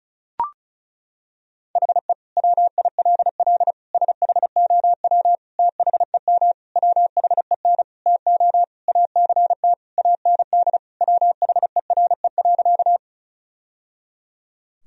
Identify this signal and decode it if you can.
Morse code, heard as a keyed tone, about 34 wpm, 700 Hz. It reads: HE WILL SHOW THEM WHEN TO ACT AND WHERE.
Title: Sentences from Top 1000 Words - Review Part 1 - 35wpm